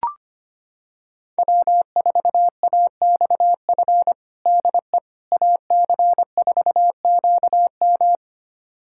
Morse code, heard as a keyed tone, about 25 wpm, 700 Hz. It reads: W4AXF DE AC4QM
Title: USA Callsign DE Callsign - 5 Characters - 25wpm